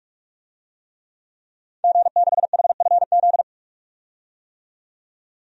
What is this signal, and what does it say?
Morse code, heard as a keyed tone, about 42 wpm, 700 Hz. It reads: G6HFZ
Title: DXCC-100 Call Signs - 5 Characters - 45wpm